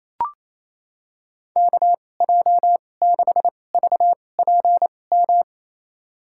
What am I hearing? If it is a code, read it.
Morse code, heard as a keyed tone, about 28 wpm, spaced as for 20 wpm, 700 Hz. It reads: KJ6VPM
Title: USA Call Signs - 6 Characters - 2x Character Spacing - 28wpm